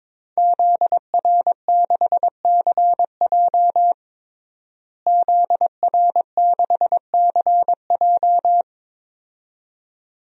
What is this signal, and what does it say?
Morse code, heard as a keyed tone, about 22 wpm, 700 Hz. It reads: ZR6CJ ZR6CJ
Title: DXCC-100 Call Signs - 5 Characters - Repeated 2x - 22wpm